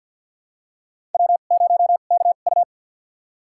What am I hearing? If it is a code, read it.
Morse code, heard as a keyed tone, about 47 wpm, spaced as for 35 wpm, 700 Hz. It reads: W0KU
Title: USA Call Signs - 4 Characters - 2x Character Spacing - 50wpm